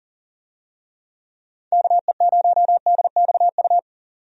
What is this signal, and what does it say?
Morse code, heard as a keyed tone, about 39 wpm, 700 Hz. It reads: KE0DXU
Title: USA Call Signs - 6 Characters - 40wpm